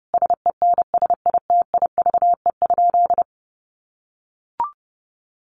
Morse code, heard as a keyed tone, about 30 wpm, 700 Hz. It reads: SENSITIVE?
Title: Sentences from Top 1400 Words 30wpm